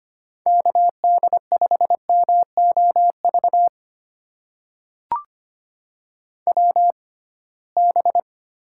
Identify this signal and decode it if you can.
Morse code, heard as a keyed tone, about 25 wpm, spaced as for 19 wpm, 700 Hz. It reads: KD5MOV WB
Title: USA Call Signs - 6 Characters - 6x Character Spacing - 25wpm